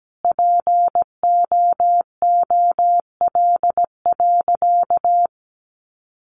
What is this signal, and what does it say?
Morse code, heard as a keyed tone, about 17 wpm, 700 Hz. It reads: POOL.
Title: Additional Easy Phrases 17wpm